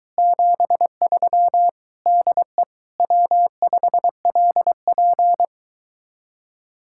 Morse code, heard as a keyed tone, about 23 wpm, 700 Hz. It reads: 73 DE W5LP